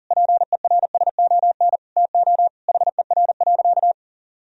Morse code, heard as a keyed tone, about 40 wpm, 700 Hz. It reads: PERSON TO HER.
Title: Sentences from Top 1000 Words - Review Part 3 - 40wpm